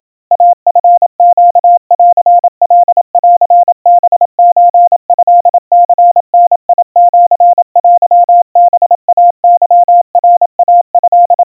Morse code (Japanese Wabun code), heard as a keyed tone, about 27 wpm, 700 Hz. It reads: イチネンカンハソトニダシテハイケナイト
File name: kawa / Morse sample / kamyu 100 cpm 700 Hz Crispy